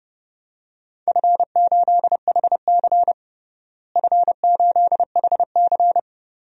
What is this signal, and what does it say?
Morse code, heard as a keyed tone, about 32 wpm, 700 Hz. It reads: F8HC F8HC
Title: DXCC-100 Call Signs - 4 Characters - Repeated 3x - 30wpm